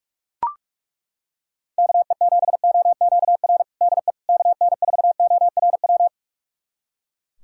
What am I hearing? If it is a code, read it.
Morse code, heard as a keyed tone, about 41 wpm, 700 Hz. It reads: KE7OQR DE KN4ORW
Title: USA Callsign DE Callsign - 6 Characters - 45wpm